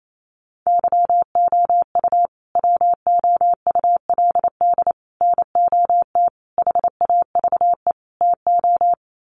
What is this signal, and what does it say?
Morse code, heard as a keyed tone, about 28 wpm, 690 Hz. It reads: YOU WOULD NOT HAVE TO
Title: Sets of 5 Words Encore 28wpm